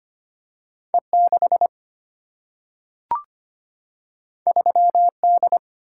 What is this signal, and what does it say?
Morse code, heard as a keyed tone, about 25 wpm, 700 Hz. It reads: E6 3D
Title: Single-Letter Single-Number 25wpm